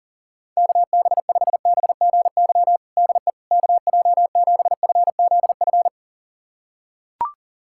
Morse code, heard as a keyed tone, about 40 wpm, 700 Hz. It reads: KB5BGY DE KJ7FZF
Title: USA Callsign DE Callsign 40wpm